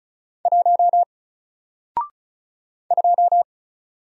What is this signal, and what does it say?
Morse code, heard as a keyed tone, about 33 wpm, 700 Hz. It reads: 1 2
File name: Single-Letter-Number-35wpm